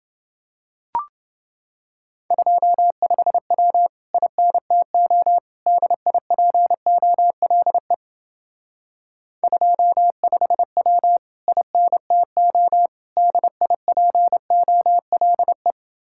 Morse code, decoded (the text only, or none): 25W INTO DIPOLE 25W INTO DIPOLE